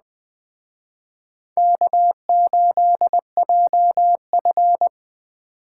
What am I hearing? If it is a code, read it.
Morse code, heard as a keyed tone, about 20 wpm, 700 Hz. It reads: K8JF
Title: Speed Racing - USA 1x2 Call Signs 17wpm